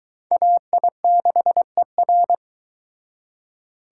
Morse code, heard as a keyed tone, about 23 wpm, 700 Hz. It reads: AI6ER